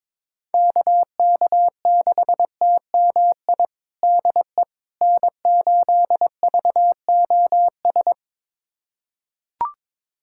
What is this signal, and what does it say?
Morse code, heard as a keyed tone, about 22 wpm, 700 Hz. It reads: KK6TMI DE N8VOS
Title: USA Callsign DE Callsign 22wpm